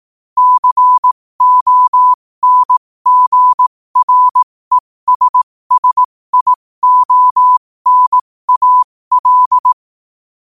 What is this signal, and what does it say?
Morse code, heard as a keyed tone, about 18 wpm, spaced as for 15 wpm, 1000 Hz. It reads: CONGRESSIONAL
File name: qotdfiles / 15wpm